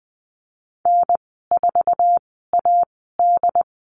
Morse code, heard as a keyed tone, about 20 wpm, spaced as for 15 wpm, 700 Hz. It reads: N4AD
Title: 082 Lesson 14g - Number 2 - Call Signs 20wpm